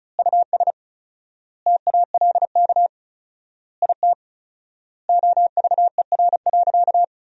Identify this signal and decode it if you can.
Morse code, heard as a keyed tone, about 36 wpm, 700 Hz. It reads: US TALK IT OVER.
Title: 4x Word-Spacing - Sentences from Top 500 Words - 35wpm